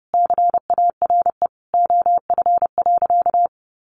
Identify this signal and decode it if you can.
Morse code, heard as a keyed tone, about 30 wpm, 700 Hz. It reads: CARE OF.
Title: Additional Easy Phrases 30wpm